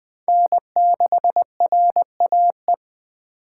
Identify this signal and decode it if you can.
Morse code, heard as a keyed tone, about 20 wpm, 700 Hz. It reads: N6RAE
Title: Speed Racing - USA 1x3 Call Signs 17wpm